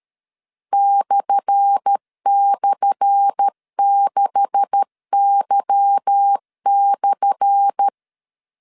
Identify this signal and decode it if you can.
Morse code, heard as a keyed tone, about 13 wpm, 800 Hz. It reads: //6Y/
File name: LEZIONI / 10 SIGNS / 10A SLASH 12wpm 2314